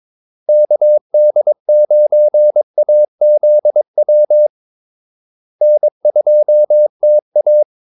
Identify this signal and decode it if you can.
Morse code, heard as a keyed tone, about 22 wpm, 600 Hz. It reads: KD9AZW N2TA
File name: Callsigns-22wpm